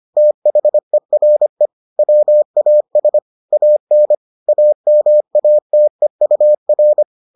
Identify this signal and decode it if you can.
Morse code, heard as a keyed tone, about 25 wpm, 600 Hz. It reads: THERE WAS AN AMATEUR